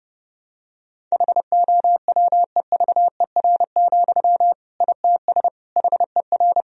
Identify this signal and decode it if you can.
Morse code, heard as a keyed tone, about 30 wpm, 700 Hz. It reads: HOWEVER, ITS HER
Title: Sentences from Top 1000 Words - Review Part 1 - 30wpm